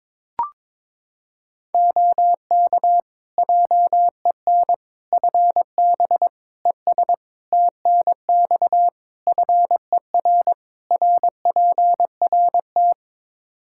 Morse code, heard as a keyed tone, about 22 wpm, 700 Hz. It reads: OK JEN FB ES TNX FER RPRT
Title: Practice QSO - Exchange 2 - Info - 22wpm